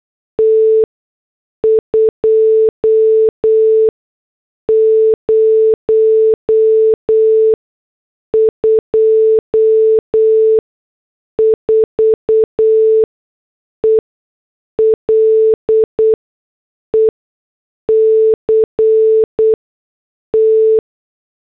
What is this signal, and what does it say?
Morse code, heard as a keyed tone, about 8 wpm, spaced as for 6 wpm, 440 Hz. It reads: T 2024 ELECT